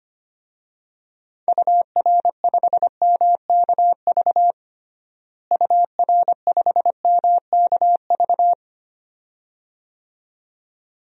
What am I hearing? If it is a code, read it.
Morse code, heard as a keyed tone, about 25 wpm, 700 Hz. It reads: UR5MKV UR5MKV
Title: DXCC-100 Call Signs - 6 Characters - Repeated 2x - 25wpm